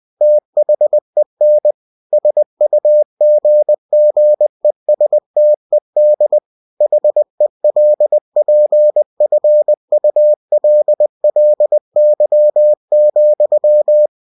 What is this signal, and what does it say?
Morse code, heard as a keyed tone, about 20 wpm, 600 Hz. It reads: THEN SUGGESTED HELPFULLY,
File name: cw-books / winnie-the-pooh / winnie-20wpm-page-0089